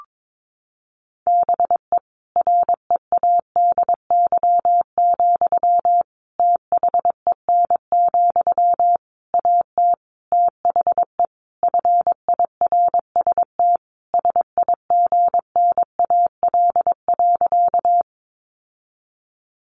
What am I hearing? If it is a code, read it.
Morse code, heard as a keyed tone, about 22 wpm, 700 Hz. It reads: BE READY, THEN, AT THE FIRST SIGNAL.
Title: Sentences from Top 1400 Words 22wpm